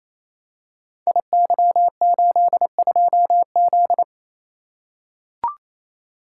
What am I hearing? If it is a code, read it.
Morse code, heard as a keyed tone, about 27 wpm, 700 Hz. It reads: IY82Z
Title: Code Groups - 5 Letter:Numbers - 28wpm